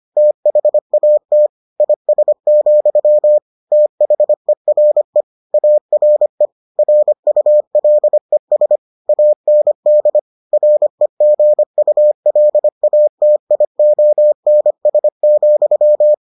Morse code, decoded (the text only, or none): THAT IS, THERE ARE RULES AND REGULATIONS,